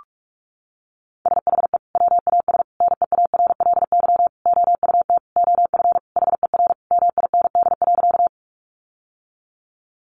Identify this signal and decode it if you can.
Morse code, heard as a keyed tone, about 44 wpm, 700 Hz. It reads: SHE WAS NEARLY OUT OF HER MIND.
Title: Sentences from Top 1300 Words 45wpm